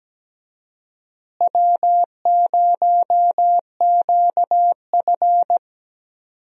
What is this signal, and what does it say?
Morse code, heard as a keyed tone, about 17 wpm, 700 Hz. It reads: W0QF
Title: USA Call Signs - 4 Characters - Repeated 2x - 17wpm